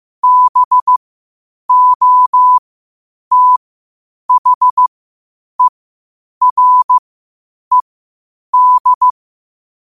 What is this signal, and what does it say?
Morse code, heard as a keyed tone, about 15 wpm, spaced as for 8 wpm, 1000 Hz. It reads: BOTHERED
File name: qotdfiles / 8wpm